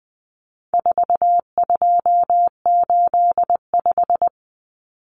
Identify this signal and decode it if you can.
Morse code, heard as a keyed tone, about 20 wpm, 700 Hz. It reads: 4285
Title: Common Cut Numbers - 4-Digit 20wpm